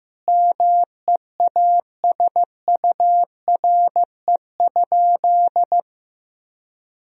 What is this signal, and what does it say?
Morse code, heard as a keyed tone, about 15 wpm, 700 Hz. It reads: MEASURE?
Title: Sentences from Top 1000 Words - Review Part 2 - 15wpm